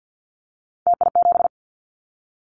Morse code, heard as a keyed tone, about 50 wpm, 700 Hz. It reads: TI7
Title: Callsign Prefixes for Top 100 DXCC - 50wpm